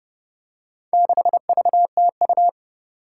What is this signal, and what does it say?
Morse code, heard as a keyed tone, about 29 wpm, 700 Hz. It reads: 6VTU
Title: Four Challenging Letter-Number Combinations 30wpm